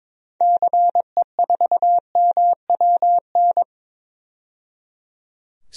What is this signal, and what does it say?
Morse code, heard as a keyed tone, about 21 wpm, 700 Hz. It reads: CE4MWN
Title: DXCC-100 Call Signs - 6 Characters - 22wpm